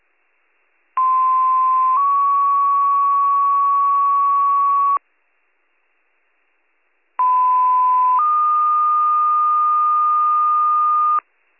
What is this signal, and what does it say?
Two-tone page, heard as a one-second tone, then a three-second tone, about 1040 then 1120 Hz, about 1010 then 1220 Hz. It is heard from 1.0 to 11.2 s.